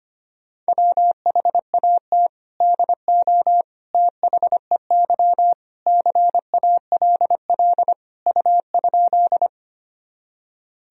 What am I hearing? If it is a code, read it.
Morse code, heard as a keyed tone, about 25 wpm, 700 Hz. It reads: WHAT DO THEY CALL U?
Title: Sentences from Top 100 Words and Abbreviations 25wpm